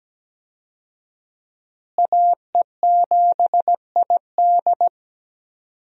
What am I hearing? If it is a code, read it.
Morse code, heard as a keyed tone, about 17 wpm, 700 Hz. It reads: AE7ID
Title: USA Call Signs - 5 Characters - Repeated 3x - 17wpm